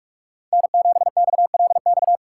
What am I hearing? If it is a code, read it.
Morse code, heard as a keyed tone, about 46 wpm, 700 Hz. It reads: N7XLX
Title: USA Call Signs - 5 Characters - 4x Character Spacing - 45wpm